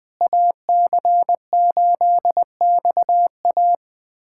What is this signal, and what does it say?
Morse code, heard as a keyed tone, about 21 wpm, 700 Hz. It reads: AC8XA
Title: USA Callsign DE Callsign - 5 Characters - 20wpm